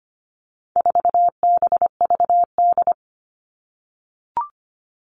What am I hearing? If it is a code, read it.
Morse code, heard as a keyed tone, about 25 wpm, 700 Hz. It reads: 4BVD
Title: All Cut Numbers - 4-Digit 25wpm